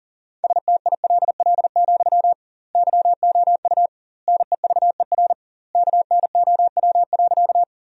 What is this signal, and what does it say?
Morse code, heard as a keyed tone, about 40 wpm, 700 Hz. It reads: STILL, YOU NEVER KNOW.
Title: 2x Word-Spacing - Sentences from Top 300 Words - 40wpm